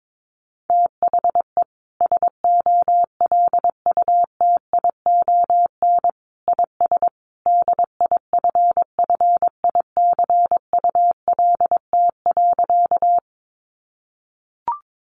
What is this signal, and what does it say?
Morse code, heard as a keyed tone, about 22 wpm, 700 Hz. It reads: THE SOLUTION IS DIFFICULT.
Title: Sentences from Top 1300 Words 22wpm